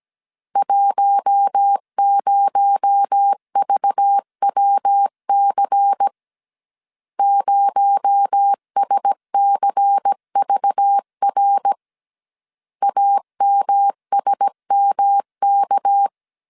Morse code, heard as a keyed tone, about 17 wpm, 800 Hz. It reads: 10VWC 0SCVR AMSMK
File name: 16wpm